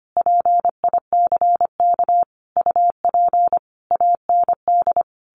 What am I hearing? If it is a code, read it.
Morse code, heard as a keyed tone, about 25 wpm, 700 Hz. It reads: PICK UP AND
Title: A Sound Of Thunder 25wpm